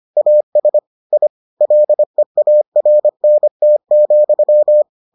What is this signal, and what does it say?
Morse code, heard as a keyed tone, about 25 wpm, 600 Hz. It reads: AS I LEARNT,